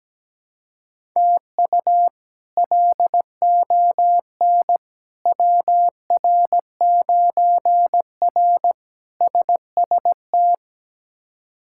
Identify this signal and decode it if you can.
Morse code, heard as a keyed tone, about 17 wpm, 700 Hz. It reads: TU LON WR9R SST
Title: K1USN SST QSO Practice - 17wpm